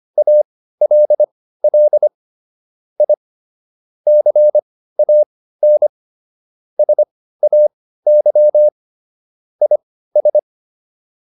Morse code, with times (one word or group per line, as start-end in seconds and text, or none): 0.2-2.1 ALL
3.0-3.1 I
4.1-5.9 CAN
6.8-8.7 SAY
9.6-10.4 IS